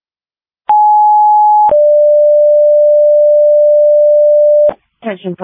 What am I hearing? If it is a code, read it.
Two-tone page, heard as a one-second tone, then a three-second tone, about 840 then 590 Hz. It is heard from 0.7 to 4.7 s.